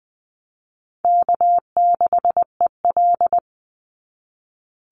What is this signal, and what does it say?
Morse code, heard as a keyed tone, about 20 wpm, 700 Hz. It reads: K6EL
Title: NA SOTA Chaser Callsigns 20wpm